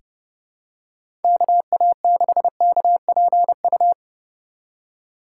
Morse code, heard as a keyed tone, about 30 wpm, 700 Hz. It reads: KA6KPU
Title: USA Call Signs - 6 Characters - 2x Character Spacing - 30wpm